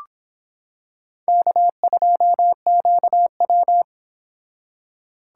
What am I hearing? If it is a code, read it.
Morse code, heard as a keyed tone, about 24 wpm, 700 Hz. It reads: K2QW